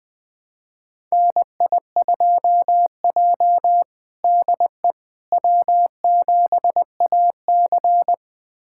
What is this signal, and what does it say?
Morse code, heard as a keyed tone, about 21 wpm, 700 Hz. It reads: NI2J DE W7AC
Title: USA Callsign DE Callsign - 4 Characters - 20wpm